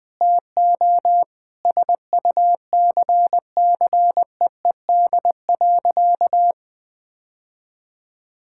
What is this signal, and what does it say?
Morse code, heard as a keyed tone, about 20 wpm, 700 Hz. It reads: TO SUCCEED.
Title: Sentences with 4-Word Phrases - 20wpm